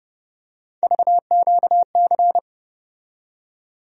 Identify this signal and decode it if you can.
Morse code, heard as a keyed tone, about 30 wpm, 700 Hz. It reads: VQC
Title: USA Call Sign Postfixes - 3 Letters 30wpm